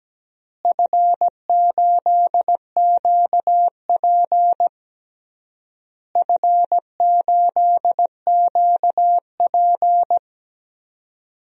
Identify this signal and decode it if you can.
Morse code, heard as a keyed tone, about 17 wpm, 700 Hz. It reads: F8QP F8QP
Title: DXCC-100 Call Signs - 4 Characters - Repeated 3x - 17wpm